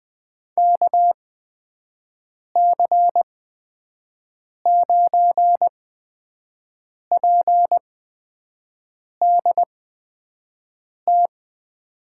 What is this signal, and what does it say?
Morse code, heard as a keyed tone, about 21 wpm, spaced as for 6 wpm, 700 Hz. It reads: KC9PDT
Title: USA Call Signs - 6 Characters - 8x Character Spacing - 20wpm